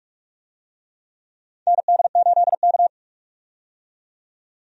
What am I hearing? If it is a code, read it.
Morse code, heard as a keyed tone, about 44 wpm, 700 Hz. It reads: ND8K
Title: USA Call Signs - 4 Characters - 45wpm